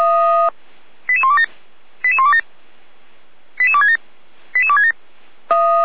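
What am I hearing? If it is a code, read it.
Five-tone selective call (ZVEI), heard as quick runs of five tones, tones of about 70 ms. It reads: 88127 88127 88267 88267
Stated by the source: ZVEI five-tone selective call received by SDR.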